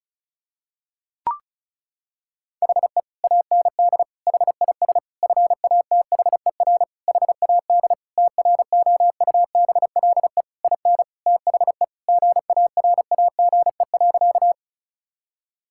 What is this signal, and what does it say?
Morse code, heard as a keyed tone, about 34 wpm, 700 Hz. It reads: HE AND HIS FATHER HAD TROUBLE IN THE GARAGE.
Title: Sentences from Top 1600 Words 35wpm